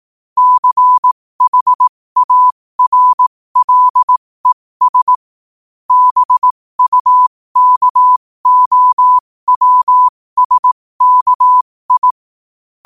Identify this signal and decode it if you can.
Morse code, heard as a keyed tone, about 18 wpm, spaced as for 15 wpm, 1000 Hz. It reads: CHARLES BUKOWSKI
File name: qotdfiles / 15wpm